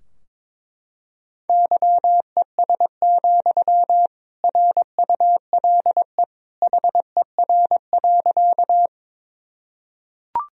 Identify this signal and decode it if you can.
Morse code, heard as a keyed tone, about 21 wpm, 700 Hz. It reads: YES, RULE HER.
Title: Sentences from Top 1000 Words - Review Part 2 - 22wpm